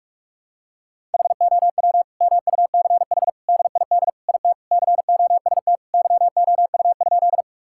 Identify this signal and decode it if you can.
Morse code, heard as a keyed tone, about 47 wpm, 700 Hz. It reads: HOW MUCH DID IT COST YOU?
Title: Sentences from Top 1000 Words - Review Part 5 - 45wpm